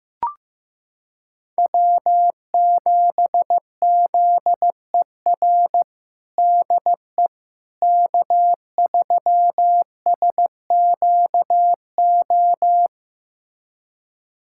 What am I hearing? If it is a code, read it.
Morse code, heard as a keyed tone, about 15 wpm, 700 Hz. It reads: W7ZER DE K3SQO
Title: USA Callsign DE Callsign - 5 Characters - 15wpm